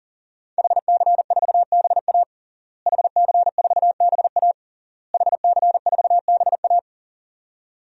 Morse code, heard as a keyed tone, about 40 wpm, 700 Hz. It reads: HC4BA HC4BA HC4BA
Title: DXCC-100 Call Signs - 5 Characters - Repeated 3x - 40wpm